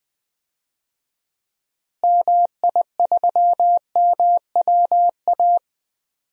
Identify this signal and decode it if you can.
Morse code, heard as a keyed tone, about 20 wpm, 700 Hz. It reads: MI3MWA
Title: DXCC-100 Call Signs - 6 Characters - Repeated 3x - 20wpm